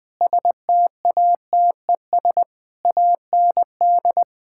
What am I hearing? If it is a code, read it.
Morse code, heard as a keyed tone, about 21 wpm, 700 Hz. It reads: STATES AND
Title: Sentences with 5-Word Phrases - 20wpm